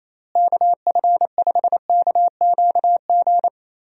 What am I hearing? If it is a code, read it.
Morse code, heard as a keyed tone, about 28 wpm, 700 Hz. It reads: KF5KQG
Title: USA Call Signs - 6 Characters - Repeated 2x - 28wpm